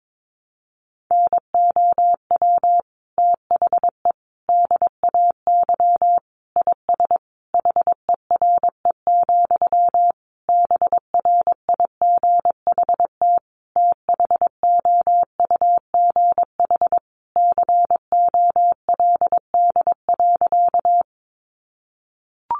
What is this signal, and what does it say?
Morse code, heard as a keyed tone, about 22 wpm, 700 Hz. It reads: NOW THE DAY IS HERE, BRIGHT THOUGH COLD.
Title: Sentences from Top 1000 Words 22wpm